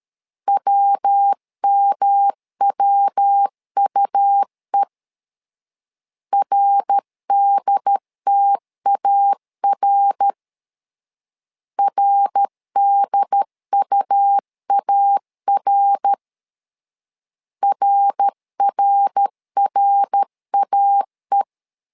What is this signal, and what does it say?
Morse code, heard as a keyed tone, about 13 wpm, 800 Hz. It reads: WMWUE RDTAR RDUAR RRRAE
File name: LEZIONI / 3 MWRO / 3 MWR 12wpm 1711